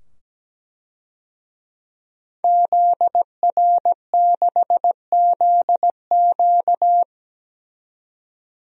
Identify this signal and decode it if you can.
Morse code, heard as a keyed tone, about 17 wpm, 700 Hz. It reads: ZR6ZQ
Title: DXCC-100 Call Signs - 5 Characters - 17wpm